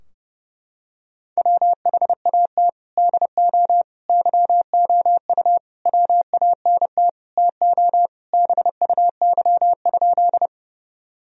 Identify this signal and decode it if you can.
Morse code, heard as a keyed tone, about 32 wpm, 700 Hz. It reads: WHAT DO YOU WANT TO BUY?
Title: Sentences from Top 1000 Words - Review Part 4 - 30wpm